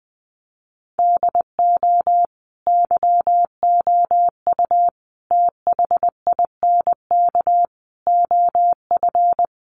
Morse code, heard as a keyed tone, about 20 wpm, 700 Hz. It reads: DO YOU THINK OF
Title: Sets of 5 Words Encore2 20wpm